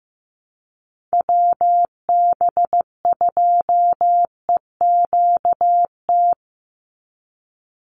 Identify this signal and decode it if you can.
Morse code, heard as a keyed tone, about 15 wpm, 700 Hz. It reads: WB2EQT
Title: USA Call Signs 15wpm